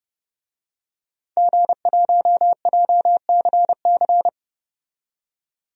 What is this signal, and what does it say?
Morse code, heard as a keyed tone, about 30 wpm, 700 Hz. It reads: G1JCC
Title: DXCC-100 Call Signs - 5 Characters - 30wpm